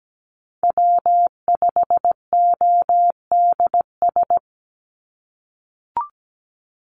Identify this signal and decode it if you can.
Morse code, heard as a keyed tone, about 17 wpm, 700 Hz. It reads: W5ODS
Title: NA SOTA Activator Callsigns 17wpm